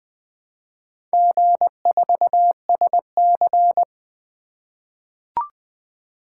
Morse code, heard as a keyed tone, about 20 wpm, 700 Hz. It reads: G4SC
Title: Four Letter-Number Combinations - 20wpm